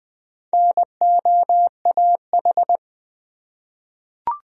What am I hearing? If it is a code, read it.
Morse code, heard as a keyed tone, about 20 wpm, 700 Hz. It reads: NOAH
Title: USA Common Names - Max 4 Letters - 20wpm